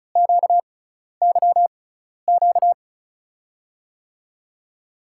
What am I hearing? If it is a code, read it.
Morse code, heard as a keyed tone, about 36 wpm, spaced as for 12 wpm, 700 Hz. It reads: QYQ
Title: USA Call Signs - 6x Character Spacing - 35wpm